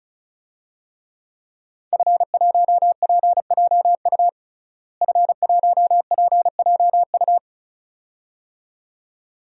Morse code, heard as a keyed tone, about 36 wpm, 700 Hz. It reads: F1PJU F1PJU
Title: DXCC-100 Call Signs - 5 Characters - Repeated 2x - 35wpm